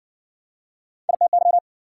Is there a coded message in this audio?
Yes: Morse code from 1.1 to 1.6 s, heard as a keyed tone, about 59 wpm, 700 Hz. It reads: ITY